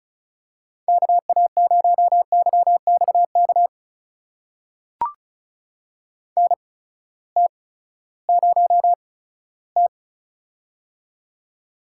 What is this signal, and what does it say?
Morse code, heard as a keyed tone, about 36 wpm, spaced as for 18 wpm, 700 Hz. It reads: KA0YXK NT0T
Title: USA Call Signs - 8x Character Spacing - 35wpm